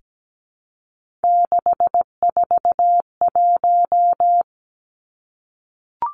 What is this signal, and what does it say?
Morse code, heard as a keyed tone, about 17 wpm, 700 Hz. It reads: 641
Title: Three Challenging Letter-Number Combinations 17wpm